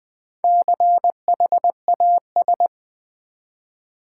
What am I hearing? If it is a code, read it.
Morse code, heard as a keyed tone, about 20 wpm, 700 Hz. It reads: CHAS
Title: Build A Word - 1 Letter At A Time - 3 to 5-Letter Words - 20wpm